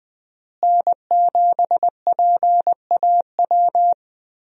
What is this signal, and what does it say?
Morse code, heard as a keyed tone, about 20 wpm, 700 Hz. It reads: N7PAW
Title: USA Call Signs - 5 Characters - Repeated 3x - 20wpm